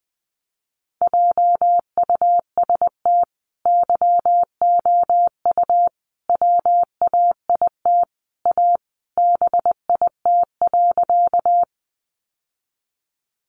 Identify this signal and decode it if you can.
Morse code, heard as a keyed tone, about 20 wpm, 700 Hz. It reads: JUST YOU WAIT A BIT.
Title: Sentences from Top 900 Words 20wpm